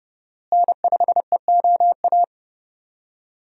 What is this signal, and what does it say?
Morse code, heard as a keyed tone, about 30 wpm, 700 Hz. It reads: N5EOA
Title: USA Call Signs - 5 Characters - Repeated 2x - 30wpm